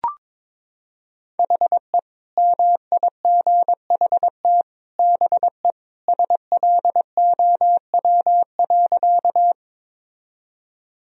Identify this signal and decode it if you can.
Morse code, heard as a keyed tone, about 21 wpm, 700 Hz. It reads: HE MIGHT BE SLOW.
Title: Sentences from Top 1000 Words - Review Part 2 - 22wpm